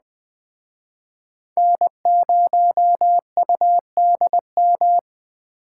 Morse code, heard as a keyed tone, about 21 wpm, 700 Hz. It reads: N0UDM